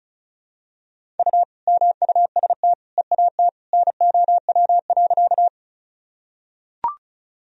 Morse code, heard as a keyed tone, about 36 wpm, 700 Hz. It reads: U MUST EAT NOW.
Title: Sentences from Top 300 Words and Abbreviations 35wpm